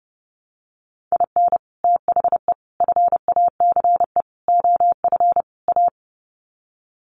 Morse code, heard as a keyed tone, about 30 wpm, 700 Hz. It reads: IN THE FACE OF A
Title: Sets of 5 Words Encore 30wpm